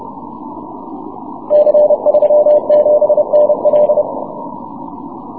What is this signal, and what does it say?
Morse code, heard as a keyed tone, about 30 wpm, 600 Hz. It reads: C37NL
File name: wp / c37nl 6m cw